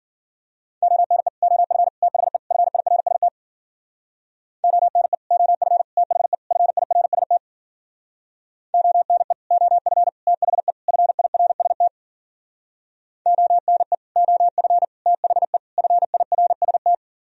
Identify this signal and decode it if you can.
Morse code, heard as a keyed tone, about 51 wpm, 700 Hz. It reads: ONE OF THE FIRST ONE OF THE FIRST ONE OF THE FIRST ONE OF THE FIRST